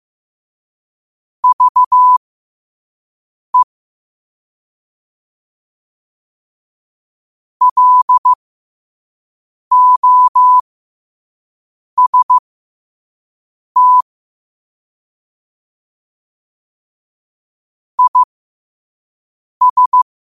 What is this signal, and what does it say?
Morse code, heard as a keyed tone, about 15 wpm, spaced as for 5 wpm, 1000 Hz. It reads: VE LOST IS
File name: qotdfiles / 5wpm